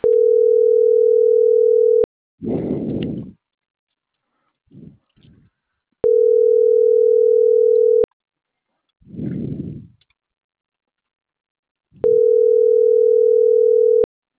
Ringback tone (North America), 440 + 480 Hz.